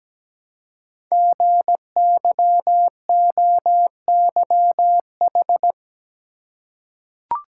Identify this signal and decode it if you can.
Morse code, heard as a keyed tone, about 17 wpm, 700 Hz. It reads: GYOYH
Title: Code Groups - 5 Letter:Numbers - 17wpm